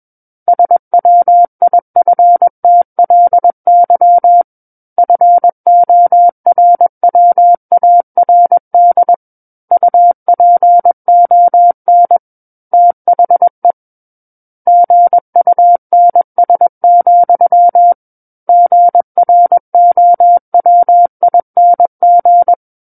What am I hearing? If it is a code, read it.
Morse code, heard as a keyed tone, about 21 wpm, 700 Hz. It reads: SWIFTLY FORWARD UPON THE GUNS, GROWING
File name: Book One / 12-Chapter 12 - 21 WPM